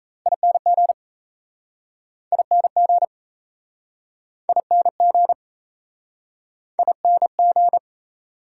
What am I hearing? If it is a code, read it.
Morse code, heard as a keyed tone, about 38 wpm, 700 Hz. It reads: ING ING ING ING